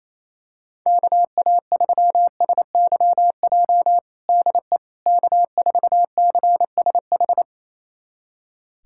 Morse code, heard as a keyed tone, about 28 wpm, 700 Hz. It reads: KA3SYJ DE K4CSH
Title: USA Callsign DE Callsign 28wpm